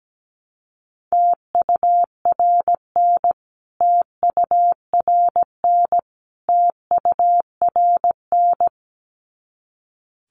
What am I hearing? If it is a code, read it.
Morse code, heard as a keyed tone, about 17 wpm, 700 Hz. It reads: TURN TURN TURN
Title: Top 200 Words Repeated 17wpm